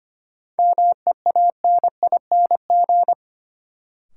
Morse code, heard as a keyed tone, about 25 wpm, 700 Hz. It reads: MEANING
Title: Sentences from Top 900 Words - New Word Drill - 25wpm